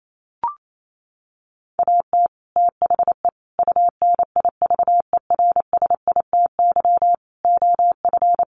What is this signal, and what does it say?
Morse code, heard as a keyed tone, about 28 wpm, 700 Hz. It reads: AT THE UNIVERSITY OF
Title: Sets of 5 Words Encore2 28wpm